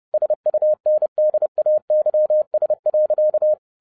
Morse code, heard as a keyed tone, about 30 wpm, 610 Hz. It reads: SUNDAYS.